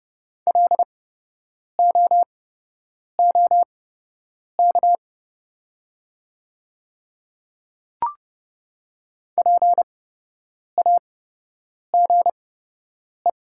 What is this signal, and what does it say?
Morse code, heard as a keyed tone, about 30 wpm, spaced as for 8 wpm, 700 Hz. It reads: LOOK PAGE
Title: 8x Character Spacing - Top 300 Words - 30wpm